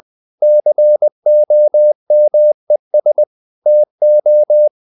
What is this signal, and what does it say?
Morse code, heard as a keyed tone, about 20 wpm, 600 Hz. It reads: COMES TO
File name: the old man and the sea part 2 20wpm